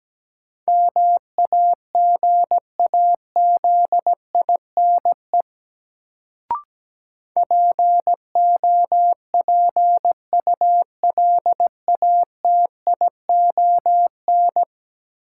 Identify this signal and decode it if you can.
Morse code, heard as a keyed tone, about 17 wpm, 700 Hz. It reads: MAGAZINE POPULATION
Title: Sentences from Top 1000 Words - Review Part 7 - New Word Drill - 17wpm